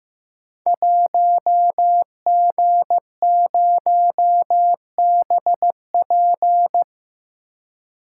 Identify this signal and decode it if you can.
Morse code, heard as a keyed tone, about 15 wpm, 700 Hz. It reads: 1G0BP
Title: Code Groups - 5 Letter:Numbers - 15wpm